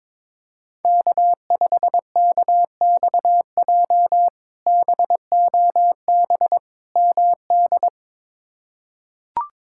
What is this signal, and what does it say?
Morse code, heard as a keyed tone, about 22 wpm, 700 Hz. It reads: K5KXJ BOB MD
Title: K1USN SST QSO Practice - 22wpm